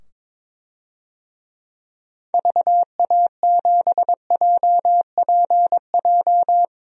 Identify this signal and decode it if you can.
Morse code, heard as a keyed tone, about 22 wpm, 700 Hz. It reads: VA7JPJ
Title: DXCC-100 Call Signs - 6 Characters - 22wpm